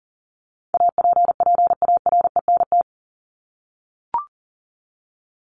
Morse code, heard as a keyed tone, about 40 wpm, 700 Hz. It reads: APPARENT
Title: LY Suffix - Top 100 Words with Base Words 40wpm